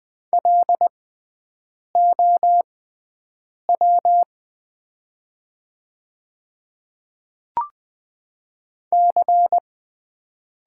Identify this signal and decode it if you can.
Morse code, heard as a keyed tone, about 20 wpm, spaced as for 4 wpm, 700 Hz. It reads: LOWC
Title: 6x Character Spacing - Top 300 Words - 20wpm